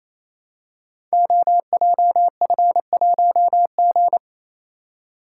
Morse code, heard as a keyed tone, about 29 wpm, 700 Hz. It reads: OJF1G
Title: Code Groups - 5 Letter:Numbers - 28wpm